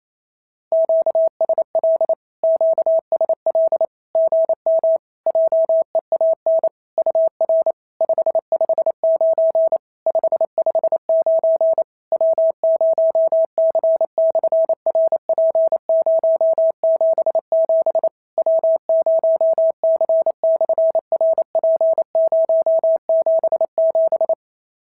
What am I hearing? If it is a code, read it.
Morse code, heard as a keyed tone, about 28 wpm, 650 Hz. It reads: QSL QSL GM JEAN UR 559 559 W0C/RP077 W0C/RP077
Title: SOTA QSO Practice 28wpm